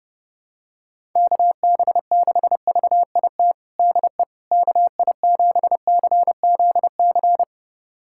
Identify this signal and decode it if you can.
Morse code, heard as a keyed tone, about 29 wpm, 700 Hz. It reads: KB6VIT DE KI7CZC
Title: USA Callsign DE Callsign - 6 Characters - 30wpm